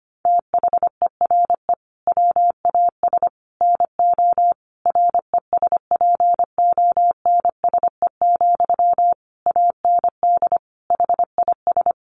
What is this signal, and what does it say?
Morse code, heard as a keyed tone, about 25 wpm, 700 Hz. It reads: THERE WAS NO RESPONSE, AND HIS
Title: Sentences from Top 1400 Words 25wpm